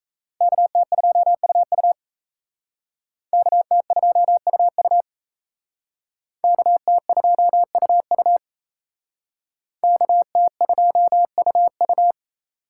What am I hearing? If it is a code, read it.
Morse code, heard as a keyed tone, about 35 wpm, 700 Hz. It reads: KT2UU KT2UU KT2UU KT2UU
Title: Speed Racing - USA Call Signs 28wpm